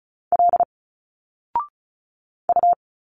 Morse code, heard as a keyed tone, about 35 wpm, 700 Hz. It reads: L U
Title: Single Letters 35wpm